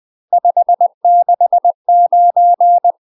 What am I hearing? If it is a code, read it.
Morse code, heard as a keyed tone, about 20 wpm, 700 Hz. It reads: 569
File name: UK MASTER QSOs TRAINING / 20WPM / UK MASTER QSOs 0035